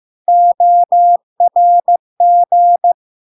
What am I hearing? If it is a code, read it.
Morse code, heard as a keyed tone, about 15 wpm, 700 Hz. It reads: ORG
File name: LARK-FD-2022-Log-in-CW-15wpm-700-Hz